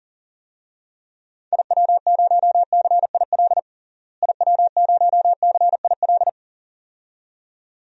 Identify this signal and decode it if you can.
Morse code, heard as a keyed tone, about 39 wpm, 700 Hz. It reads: IW0CIL IW0CIL
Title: DXCC-100 Call Signs - 6 Characters - Repeated 2x - 40wpm